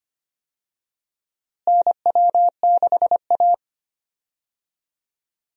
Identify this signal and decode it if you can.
Morse code, heard as a keyed tone, about 25 wpm, 700 Hz. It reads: NW6A
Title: USA Call Signs - 4 Characters - Repeated 2x - 25wpm